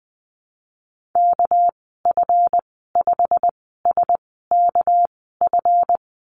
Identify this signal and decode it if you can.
Morse code, heard as a keyed tone, about 20 wpm, spaced as for 15 wpm, 700 Hz. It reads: KF5SKF
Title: 207 Lesson 33g - Letter j - Call Signs 20wpm